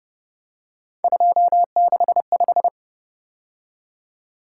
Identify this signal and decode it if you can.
Morse code, heard as a keyed tone, about 30 wpm, 700 Hz. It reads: 265
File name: Speed-Racing - Three Letter-Number Combinations - 25wpm